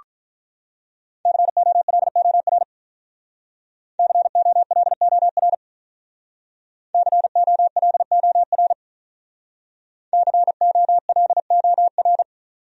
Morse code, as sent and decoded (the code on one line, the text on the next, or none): -.-. --- .-.. --- .-. / -.-. --- .-.. --- .-. / -.-. --- .-.. --- .-. / -.-. --- .-.. --- .-.
COLOR COLOR COLOR COLOR